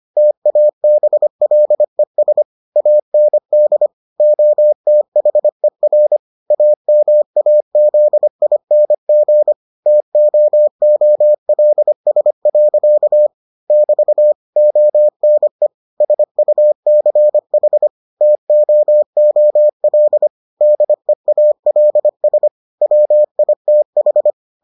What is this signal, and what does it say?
Morse code, heard as a keyed tone, about 25 wpm, 600 Hz. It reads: TABLES AND OTHER AMAZING TOOLS. <BT> ONE SUCH TOOL DEALS WITH